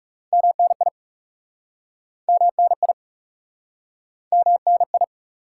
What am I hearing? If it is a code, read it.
Morse code, heard as a keyed tone, about 37 wpm, 700 Hz. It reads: MNI MNI MNI
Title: Speed-Racing Top 180 QSO Elements 30wpm